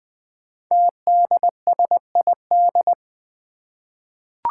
Morse code, heard as a keyed tone, about 20 wpm, 700 Hz. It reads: TDSID
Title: Five Challenging Letter-Number Combinations 20wpm